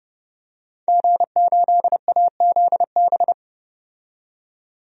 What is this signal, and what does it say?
Morse code, heard as a keyed tone, about 31 wpm, 700 Hz. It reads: G8AZB
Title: DXCC-100 Call Signs - 5 Characters - 30wpm